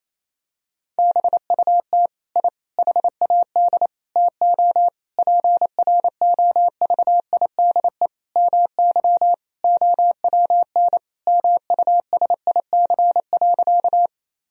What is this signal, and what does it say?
Morse code, heard as a keyed tone, about 28 wpm, 700 Hz. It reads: BUT I HAD TO PROVIDE MY OWN MUSIC.
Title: Sentences from Top 1000 Words - Review Part 3 - 28wpm